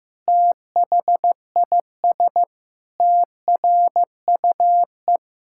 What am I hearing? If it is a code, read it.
Morse code, heard as a keyed tone, about 15 wpm, 700 Hz. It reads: THIS TRUE
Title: Pangram Challenge v2 - 15wpm